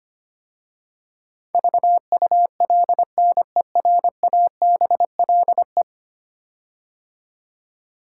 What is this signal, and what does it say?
Morse code, heard as a keyed tone, about 25 wpm, 700 Hz. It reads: VULNERABLE
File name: 10 Letter Words - First 3 Letters Given - 25wpm